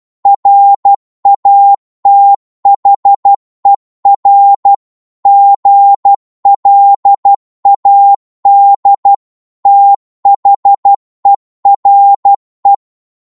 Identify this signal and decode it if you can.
Morse code, heard as a keyed tone, about 12 wpm, 800 Hz. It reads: RATHER GLAD THERE
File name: Morse / psg-12wpm